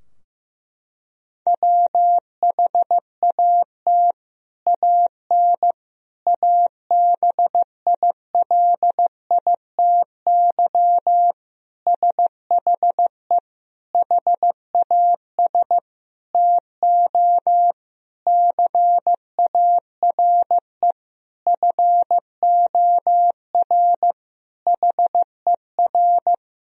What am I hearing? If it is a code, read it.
Morse code, heard as a keyed tone, about 15 wpm, 700 Hz. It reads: WHAT AN ABILITY SHE HAS TO CARE FOR HER
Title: Sentences from Top 1000 Words - Review Part 3 - 15wpm